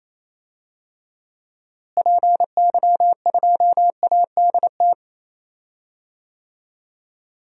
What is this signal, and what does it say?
Morse code, heard as a keyed tone, about 28 wpm, 700 Hz. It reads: PY2ADT
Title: DXCC-100 Call Signs - 6 Characters - Repeated 2x - 28wpm